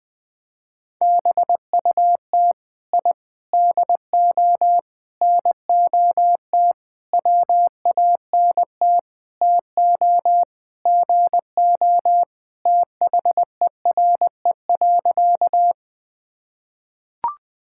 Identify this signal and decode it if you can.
Morse code, heard as a keyed tone, about 20 wpm, 700 Hz. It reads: BUT I DO NOT WANT TO GO THERE.
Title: Sentences with 5-Word Phrases - 20wpm